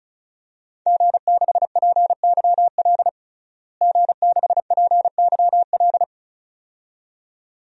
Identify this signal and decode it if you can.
Morse code, heard as a keyed tone, about 36 wpm, 700 Hz. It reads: G6PYL G6PYL
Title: DXCC-100 Call Signs - 5 Characters - Repeated 2x - 35wpm